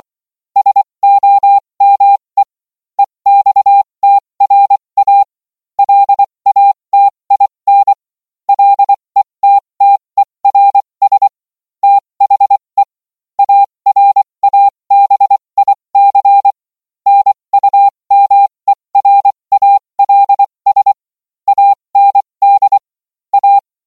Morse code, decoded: SOME EXTRA LATIN LETTERS THE ARABIC NUMERALS AND A